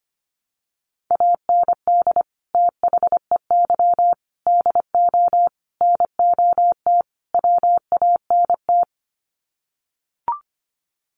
Morse code, heard as a keyed tone, about 25 wpm, 700 Hz. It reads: AND THEY DO NOT WANT
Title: Sets of 5 Words Encore2 25wpm